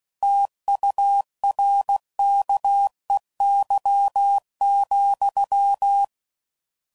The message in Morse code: TURKEY,

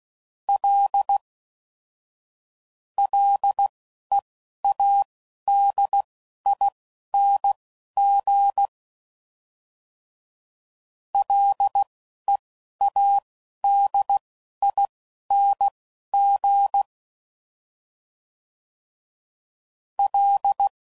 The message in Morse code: L LEADING LEADING L

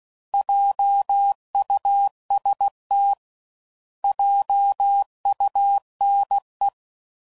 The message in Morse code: JUST JUNE

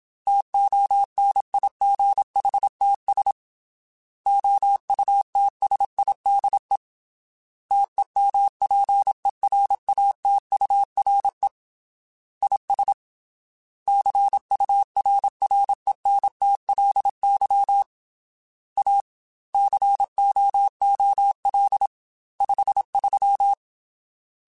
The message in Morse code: TONIGHTS OUTSIDE TEMPERATURE IS CURRENTLY A COOL 53

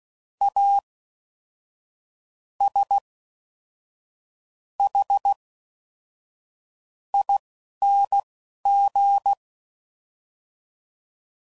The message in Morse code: A S H ING

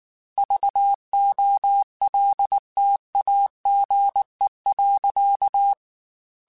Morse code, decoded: VOLTAGE.

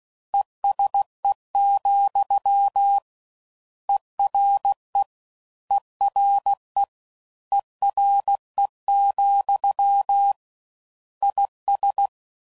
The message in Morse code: ESE, ERE ERE ERE, IS

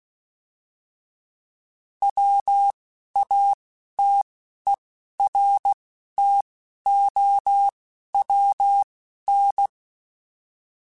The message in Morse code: WATERTOWN

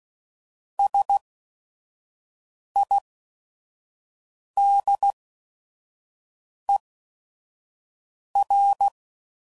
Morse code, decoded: S I D E R

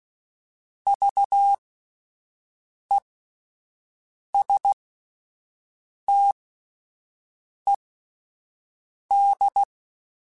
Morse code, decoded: VESTED

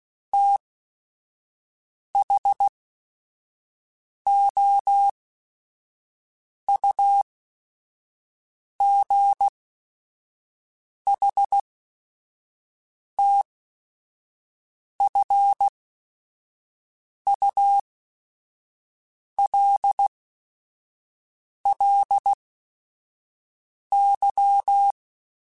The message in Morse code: THOUGHTFULLY

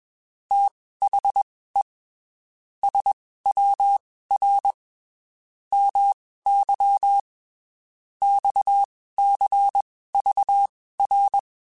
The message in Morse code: THE SWR MY XCVR

